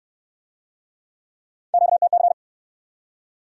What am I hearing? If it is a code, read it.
Morse code, heard as a keyed tone, about 55 wpm, 700 Hz. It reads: QTC
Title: Speed-Racing Q-Codes - 45wpm